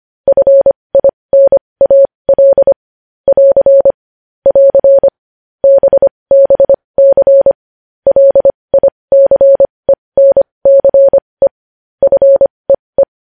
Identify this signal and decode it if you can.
Morse code, heard as a keyed tone, about 25 wpm, spaced as for 20 wpm, 550 Hz. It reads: FINAL <AR> <AR> BBC LICENCE FEE
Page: news headlines in morse code at 20 wpm